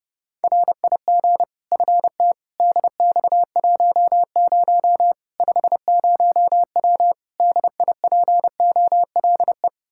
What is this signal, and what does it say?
Morse code, heard as a keyed tone, about 30 wpm, 700 Hz. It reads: RIG FT DX10 50W DIPOLE
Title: K1USN SST QSO Practice 30wpm